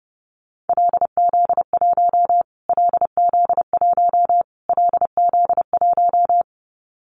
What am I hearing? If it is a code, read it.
Morse code, heard as a keyed tone, about 28 wpm, 700 Hz. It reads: LZ1 LZ1 LZ1
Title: Callsign Prefixes for Top 25 DXCC - Repeated 3x - 30wpm